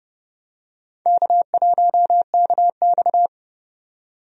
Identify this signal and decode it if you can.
Morse code, heard as a keyed tone, about 29 wpm, 700 Hz. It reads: K1KX
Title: USA Call Signs - 4 Characters - 30wpm